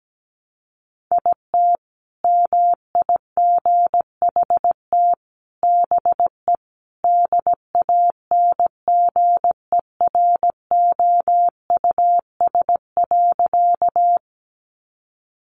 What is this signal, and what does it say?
Morse code, heard as a keyed tone, about 17 wpm, 700 Hz. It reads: IT MIGHT BE DANGEROUS.
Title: Sentences from Top 1200 Words 17wpm